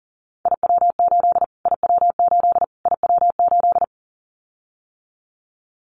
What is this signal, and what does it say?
Morse code, heard as a keyed tone, about 40 wpm, 700 Hz. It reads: IW8 IW8 IW8
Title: Callsign Prefixes for Top 25 DXCC - Repeated 3x - 40wpm